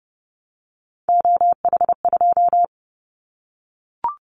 Morse code, heard as a keyed tone, about 30 wpm, 700 Hz. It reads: OH2
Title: Callsign Prefixes for Top 50 DXCC - 30wpm